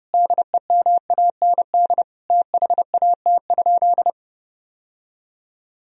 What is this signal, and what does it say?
Morse code, heard as a keyed tone, about 30 wpm, 700 Hz. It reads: DEMAND THAT?
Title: Sentences from Top 1000 Words - Review Part 4 - 30wpm